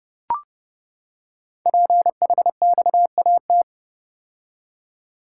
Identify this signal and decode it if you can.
Morse code, heard as a keyed tone, about 30 wpm, 700 Hz. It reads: PHXAT
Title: Code Groups - 5 Letter:Numbers - 30wpm